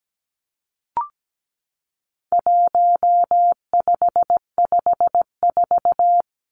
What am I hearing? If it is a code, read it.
Morse code, heard as a keyed tone, about 17 wpm, 700 Hz. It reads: 1554
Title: Numbers - 4-Digit 17wpm